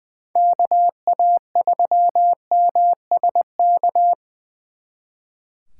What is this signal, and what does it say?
Morse code, heard as a keyed tone, about 24 wpm, 700 Hz. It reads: KA3MSK
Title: Speed Racing - USA 2x3 Call Signs 20wpm